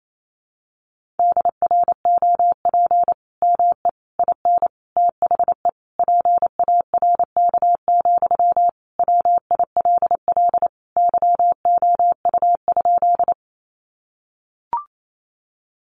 Morse code, heard as a keyed tone, about 28 wpm, 700 Hz. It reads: DROP ME IN THE PARK, WILL YOU?
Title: Sentences from Top 1400 Words 28wpm